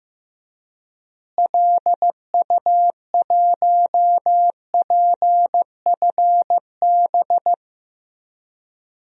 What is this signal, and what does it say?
Morse code, heard as a keyed tone, about 16 wpm, 700 Hz. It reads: LU1PFB
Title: DXCC-100 Call Signs - 6 Characters - Repeated 2x - 15wpm